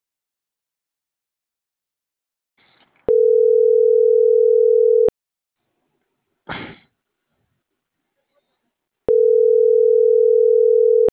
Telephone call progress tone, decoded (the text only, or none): ringback tone